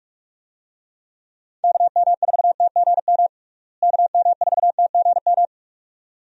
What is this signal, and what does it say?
Morse code, heard as a keyed tone, about 46 wpm, 700 Hz. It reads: KM4TGM KM4TGM
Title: USA Call Signs - 6 Characters - Repeated 2x - 45wpm